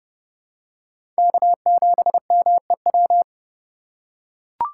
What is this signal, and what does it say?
Morse code, heard as a keyed tone, about 30 wpm, 700 Hz. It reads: K7MEW
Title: USA Call Signs - 3x Character Spacing - 30wpm